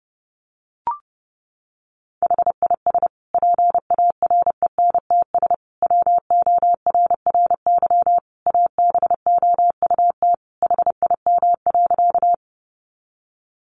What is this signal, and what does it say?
Morse code, heard as a keyed tone, about 30 wpm, 700 Hz. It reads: HIS PARENTS WORRY ABOUT HIM.
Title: Easy Phrases 30wpm